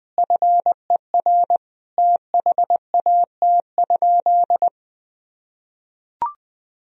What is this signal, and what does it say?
Morse code, heard as a keyed tone, about 19 wpm, 700 Hz. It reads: FER THAT?
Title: Sentences from Top 300 Words and Abbreviations 20wpm